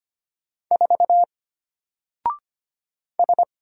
Morse code, heard as a keyed tone, about 25 wpm, 700 Hz. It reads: 4 S
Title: Single Character (Letter:Number:Punctuation:Prosign) Alt 25wpm